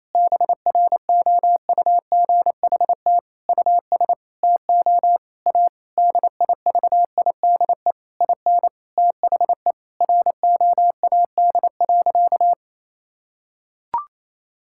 Morse code, heard as a keyed tone, about 28 wpm, 700 Hz. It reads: BROUGHT US TO A DIVIDE IN THE ROAD.
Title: Sentences from Top 1600 Words 28wpm